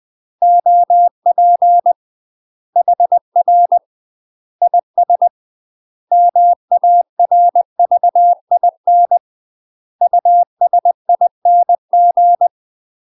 Morse code, heard as a keyed tone, about 20 wpm, 700 Hz. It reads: OP HR IS MARVIN USING
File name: UK MASTER QSOs TRAINING/20WPM